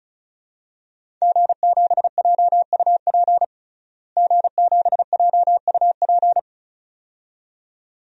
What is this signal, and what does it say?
Morse code, heard as a keyed tone, about 36 wpm, 700 Hz. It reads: G7JUP G7JUP
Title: DXCC-100 Call Signs - 5 Characters - Repeated 2x - 35wpm